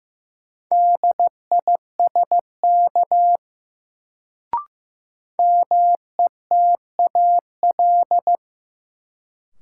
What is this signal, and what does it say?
Morse code, heard as a keyed tone, about 15 wpm, 700 Hz. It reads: DISK METAL
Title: Sentences from Top 1000 Words - Review Part 6 - New Word Drill - 15wpm